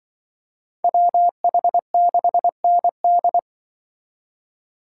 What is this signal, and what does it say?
Morse code, heard as a keyed tone, about 24 wpm, 700 Hz. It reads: WH6ND
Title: Speed Racing - USA Call Signs 20wpm